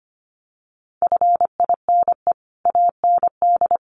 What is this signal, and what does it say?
Morse code, heard as a keyed tone, about 25 wpm, 700 Hz. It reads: FINE AND
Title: Sentences from Top 900 Words 25wpm